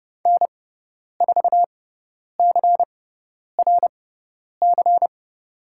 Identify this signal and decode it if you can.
Morse code, heard as a keyed tone, about 30 wpm, spaced as for 10 wpm, 690 Hz. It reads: N4CRC
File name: Lesson 17g - Letter u - Call Signs 30wpm